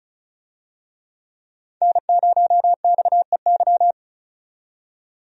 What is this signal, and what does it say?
Morse code, heard as a keyed tone, about 34 wpm, 700 Hz. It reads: N0XEY
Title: USA Call Signs - 5 Characters - 35wpm